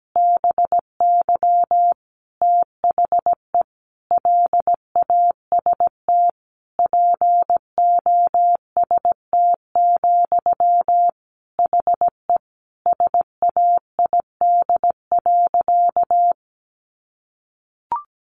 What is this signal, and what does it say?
Morse code, heard as a keyed tone, about 17 wpm, 700 Hz. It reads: BY THE LAST POST, HE SAID.
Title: Sentences from Top 1000 Words 17wpm